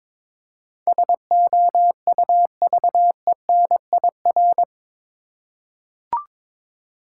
Morse code, heard as a keyed tone, about 22 wpm, 700 Hz. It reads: SOUVENIR
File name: Difficult Words with Hints - 22wpm